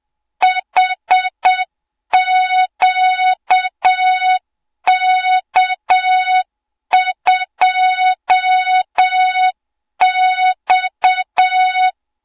Morse code, heard as a keyed tone, about 7 wpm, 3700 Hz, with noise. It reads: HQK2X
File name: Morze-HA2MN / hang / 20 GYAKORLAT